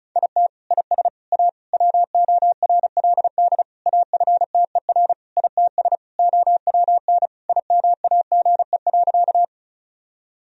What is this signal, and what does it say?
Morse code, heard as a keyed tone, about 36 wpm, 700 Hz. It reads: IT IS A WORLD AFTER ITS OWN IMAGE.
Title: Sentences from Top 1000 Words - Review Part 4 - 35wpm